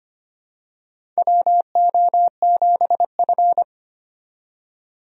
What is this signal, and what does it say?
Morse code, heard as a keyed tone, about 25 wpm, 700 Hz. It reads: WO7F
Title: USA Call Signs - 4 Characters - 25wpm